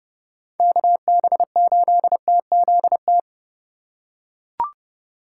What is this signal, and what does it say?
Morse code, heard as a keyed tone, about 30 wpm, 700 Hz. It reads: KB8TZT
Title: USA Call Signs - 6 Characters - 3x Character Spacing - 30wpm